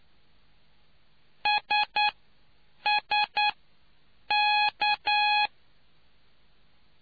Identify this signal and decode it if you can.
Morse code, heard as a keyed tone, about 9 wpm, spaced as for 7 wpm, 840 Hz. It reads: SSK